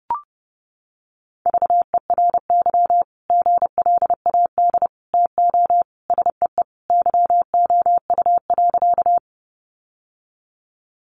Morse code, heard as a keyed tone, about 30 wpm, 700 Hz. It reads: VERY GLAD TO SEE YOU.
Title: Sentences from Top 800 Words 30wpm